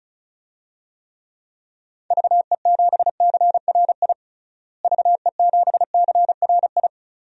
Morse code, heard as a keyed tone, about 36 wpm, 700 Hz. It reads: VE7CRI VE7CRI
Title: DXCC-100 Call Signs - 6 Characters - Repeated 3x - 35wpm